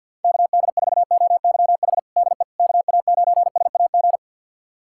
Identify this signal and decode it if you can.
Morse code, heard as a keyed tone, about 52 wpm, 700 Hz. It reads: KD4OYH DE KA9SAG
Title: USA Callsign DE Callsign - 6 Characters - 50wpm